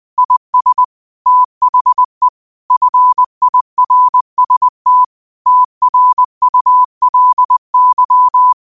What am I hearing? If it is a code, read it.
Morse code, heard as a keyed tone, about 20 wpm, 1000 Hz. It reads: IS THE FIRST TRULY